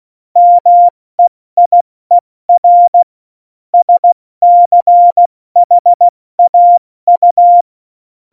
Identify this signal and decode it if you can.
Morse code, heard as a keyed tone, about 16 wpm, 700 Hz. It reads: MEIER SCHAU